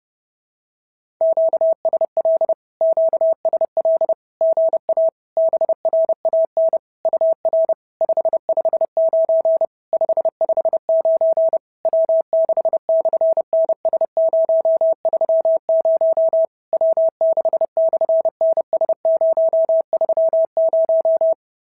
Morse code, heard as a keyed tone, about 30 wpm, 650 Hz. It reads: QSL QSL GA BRAN UR 559 559 W6/NS030 W6/NS030
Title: SOTA QSO Practice 30wpm